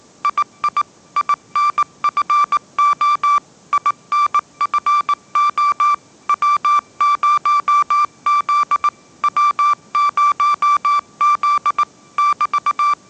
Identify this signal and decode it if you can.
Morse code, heard as a keyed tone, about 19 wpm, 1170 Hz. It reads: II INFO INFO W0Z W0Z <BT>
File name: 20wpm